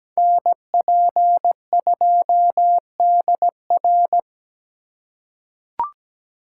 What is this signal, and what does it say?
Morse code, heard as a keyed tone, about 17 wpm, 700 Hz. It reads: NP2DR
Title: USA Call Signs - 4x Character Spacing - 17wpm